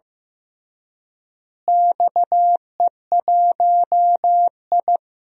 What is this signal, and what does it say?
Morse code, heard as a keyed tone, about 15 wpm, 700 Hz. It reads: XE1I
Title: DXCC-100 Call Signs - 4 Characters - Repeated 3x - 15wpm